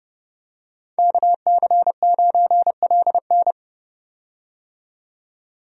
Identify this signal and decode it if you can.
Morse code, heard as a keyed tone, about 30 wpm, 700 Hz. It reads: KC9LN
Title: USA Call Signs - 5 Characters - 30wpm